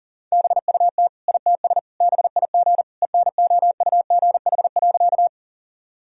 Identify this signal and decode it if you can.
Morse code, heard as a keyed tone, about 37 wpm, 700 Hz. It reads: BUT ITS BIG ENOUGH.
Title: Sentences from Top 1000 Words - Review Part 1 - 40wpm